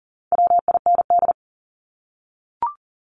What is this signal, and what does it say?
Morse code, heard as a keyed tone, about 40 wpm, 700 Hz. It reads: WIND
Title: Top 500 Words Repeated 40wpm